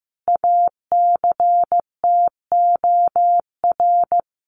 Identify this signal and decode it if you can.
Morse code, heard as a keyed tone, about 15 wpm, 700 Hz. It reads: ACTOR
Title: Sentences from Top 1500 Words 15wpm